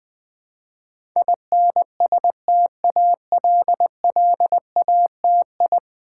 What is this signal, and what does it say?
Morse code, heard as a keyed tone, about 20 wpm, 700 Hz. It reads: INSTALLATI
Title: Build A Word - 5 Letters At A Time - 12-Letter Words - 20wpm